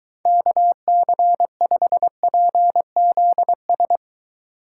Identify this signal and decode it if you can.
Morse code, heard as a keyed tone, about 21 wpm, 700 Hz. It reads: KC5PZS